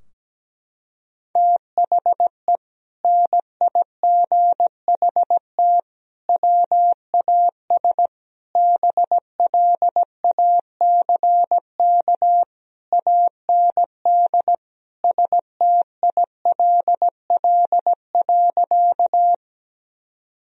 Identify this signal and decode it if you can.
Morse code, heard as a keyed tone, about 17 wpm, 700 Hz. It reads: THE NIGHT WAS BLACK AND STILL.
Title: Sentences from Top 1000 Words - Review Part 2 - 17wpm